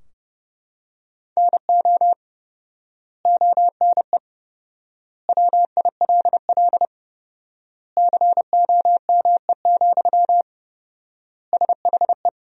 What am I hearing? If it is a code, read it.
Morse code, heard as a keyed tone, about 32 wpm, 700 Hz. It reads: NO ONE WILL COME, SHE
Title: 4x Word-Spacing - Sentences from Top 100 Words - 30wpm